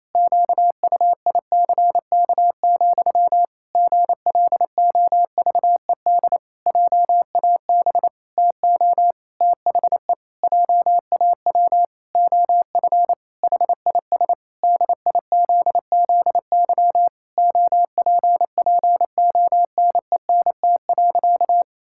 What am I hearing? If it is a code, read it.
Morse code, heard as a keyed tone, about 28 wpm, 700 Hz. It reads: QUICK, GLOVED JAB TO THE JAW OF HIS DIZZY OPPONENT.
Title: Pangram Challenge v2 - 28wpm